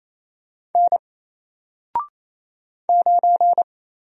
Morse code, heard as a keyed tone, about 28 wpm, 700 Hz. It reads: N 9
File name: Single-Letter-Number-28wpm